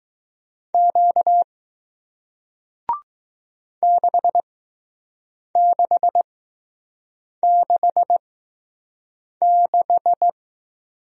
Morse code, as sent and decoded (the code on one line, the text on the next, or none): --.- / -.... -.... -.... -....
Q 6666